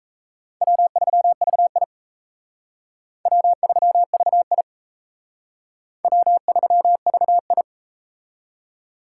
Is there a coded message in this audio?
Yes: Morse code from 0.6 to 7.6 s, heard as a keyed tone, about 39 wpm, 700 Hz. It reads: W3VI W3VI W3VI